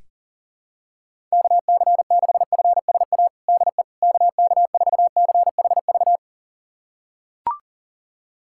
Morse code, heard as a keyed tone, about 40 wpm, 700 Hz. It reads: KC6FSA DE KK4CHV
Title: USA Callsign DE Callsign 40wpm